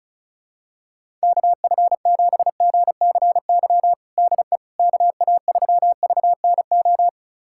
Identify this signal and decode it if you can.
Morse code, heard as a keyed tone, about 36 wpm, 700 Hz. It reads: KF7GCY DE KA3VNO
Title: USA Callsign DE Callsign - 6 Characters - 35wpm